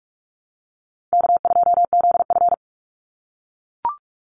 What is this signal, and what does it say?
Morse code, heard as a keyed tone, about 44 wpm, 700 Hz. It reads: K2ZF
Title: Call Signs - 45wpm